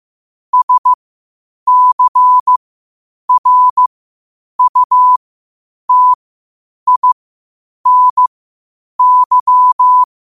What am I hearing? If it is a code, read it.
Morse code, heard as a keyed tone, about 15 wpm, spaced as for 8 wpm, 1000 Hz. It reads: SCRUTINY
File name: qotdfiles / 8wpm